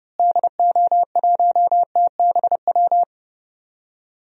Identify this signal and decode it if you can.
Morse code, heard as a keyed tone, about 32 wpm, 700 Hz. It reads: DO1TBW
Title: DXCC-100 Call Signs - 6 Characters - Repeated 2x - 30wpm